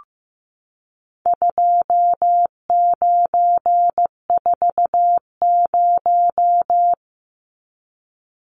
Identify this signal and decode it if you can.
Morse code, heard as a keyed tone, about 15 wpm, 700 Hz. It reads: 2940
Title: Numbers - 4-Digit 15wpm